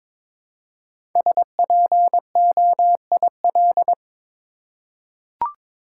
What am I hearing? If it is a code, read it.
Morse code, heard as a keyed tone, about 22 wpm, 700 Hz. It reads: SPOIL
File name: 5 Letter Words - First 3 Letters Given - 22wpm